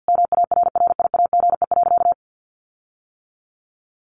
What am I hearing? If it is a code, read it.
Morse code, heard as a keyed tone, about 50 wpm, 700 Hz. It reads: MARRIAGE.
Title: Sentences from Top 1400 Words 50wpm